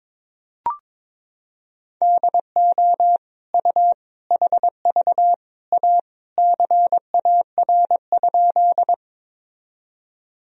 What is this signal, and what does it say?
Morse code, heard as a keyed tone, about 22 wpm, 700 Hz. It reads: DO U HV A CAR?
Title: Sentences from Top 300 Words and Abbreviations 22wpm